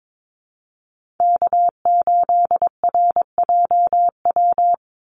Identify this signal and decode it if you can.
Morse code, heard as a keyed tone, about 22 wpm, 700 Hz. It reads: K8RJW
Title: Call Signs Repeated 3x - 22wpm